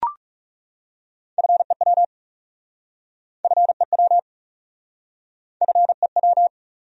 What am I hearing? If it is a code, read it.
Morse code, heard as a keyed tone, about 37 wpm, 700 Hz. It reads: FEW FEW FEW
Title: Speed-Racing Top 500 Words 30wpm